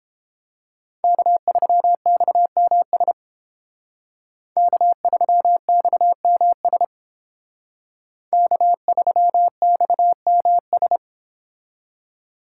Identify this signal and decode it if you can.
Morse code, heard as a keyed tone, about 31 wpm, 700 Hz. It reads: K3XMS K3XMS K3XMS